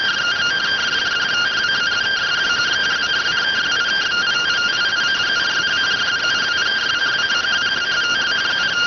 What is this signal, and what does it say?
Two-tone teleprinter signal, about 170 Hz shift.